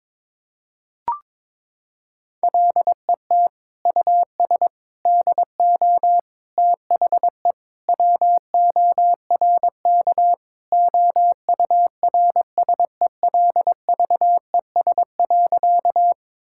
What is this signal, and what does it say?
Morse code, heard as a keyed tone, about 22 wpm, 700 Hz. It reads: LET US DO THE WORK OURSELVES.
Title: Sentences from Top 1600 Words 22wpm